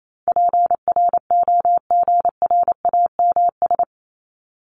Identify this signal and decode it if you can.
Morse code, heard as a keyed tone, about 28 wpm, 700 Hz. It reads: PROGRAMS
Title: ES:S Suffix - Top 100 Words with Base Words 28wpm